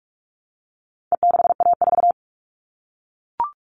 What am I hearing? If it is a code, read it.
Morse code, heard as a keyed tone, about 44 wpm, 700 Hz. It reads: E6A4
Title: All Cut Numbers - 4-Digit 45wpm